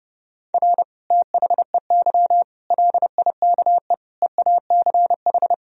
Morse code, heard as a keyed tone, about 30 wpm, 700 Hz. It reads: R THEY LIKE EACH
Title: Sentences from Top 200 Words and Abbreviations 30wpm